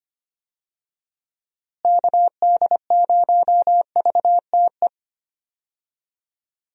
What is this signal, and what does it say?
Morse code, heard as a keyed tone, about 25 wpm, 700 Hz. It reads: KD0VTE
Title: USA Call Signs - 6 Characters - Repeated 3x - 25wpm